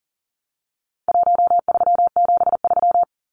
Morse code, heard as a keyed tone, about 40 wpm, 700 Hz. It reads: 1373
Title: Numbers - Contest Style - 4-Digit 40wpm